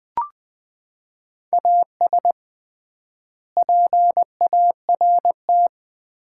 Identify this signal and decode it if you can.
Morse code, heard as a keyed tone, about 21 wpm, 700 Hz. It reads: AS PART
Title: 3x Word-Spacing - Sets of 3 Words - 20wpm